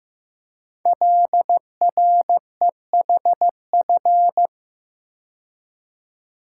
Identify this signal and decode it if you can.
Morse code, heard as a keyed tone, about 15 wpm, 700 Hz. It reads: LREHF
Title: Code Groups - 5 Letters - 15wpm